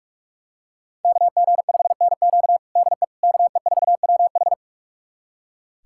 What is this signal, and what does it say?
Morse code, heard as a keyed tone, about 46 wpm, 690 Hz. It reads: KG5NQ DE KE4WH
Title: USA Callsign DE Callsign 45wpm